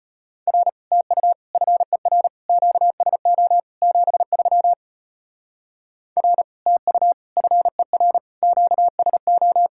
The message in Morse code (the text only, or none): R TU FER QSO 73 R TU FER QSO